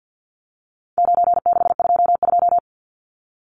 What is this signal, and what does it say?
Morse code, heard as a keyed tone, about 50 wpm, 700 Hz. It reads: 9622
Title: Numbers - 4-Digit 50wpm